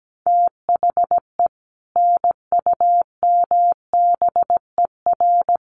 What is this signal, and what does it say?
Morse code, heard as a keyed tone, about 17 wpm, 700 Hz. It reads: THE NUMBER
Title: Sets of 3 Words 17wpm